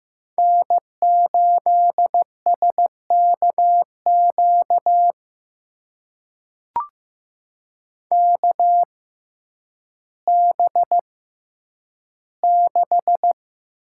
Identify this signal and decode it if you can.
Morse code, heard as a keyed tone, about 15 wpm, spaced as for 9 wpm, 700 Hz. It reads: N8SKQ KB6
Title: USA Call Signs - 5 Characters - 6x Character Spacing - 15wpm